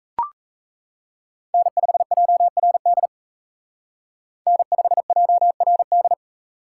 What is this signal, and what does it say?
Morse code, heard as a keyed tone, about 33 wpm, 700 Hz. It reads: N5JRD N5JRD
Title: Speed Racing - USA 1x3 Call Signs 28wpm